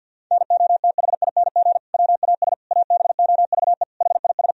Morse code, heard as a keyed tone, about 50 wpm, 700 Hz. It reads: NOTHING WAS ABOVE HIS
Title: Sentences from Top 1000 Words - Review Part 2 - 50wpm